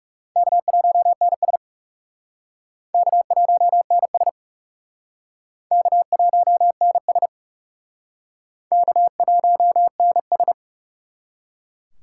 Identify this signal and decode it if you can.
Morse code, heard as a keyed tone, about 37 wpm, 700 Hz. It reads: K1NS K1NS K1NS K1NS
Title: Speed Racing - USA 1x2 Call Signs 30wpm